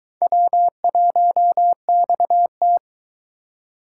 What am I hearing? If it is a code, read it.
Morse code, heard as a keyed tone, about 23 wpm, 700 Hz. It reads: W1XT